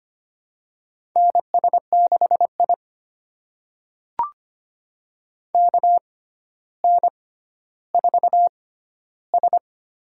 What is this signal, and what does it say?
Morse code, heard as a keyed tone, about 25 wpm, spaced as for 13 wpm, 700 Hz. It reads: NS6I KN4S
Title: USA Call Signs - 4 Characters - 6x Character Spacing - 25wpm